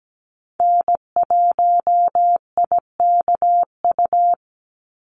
Morse code, heard as a keyed tone, about 17 wpm, 700 Hz. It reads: N1IKU
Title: USA Call Signs 17wpm